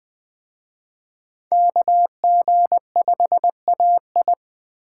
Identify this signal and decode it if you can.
Morse code, heard as a keyed tone, about 21 wpm, 700 Hz. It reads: KG5AI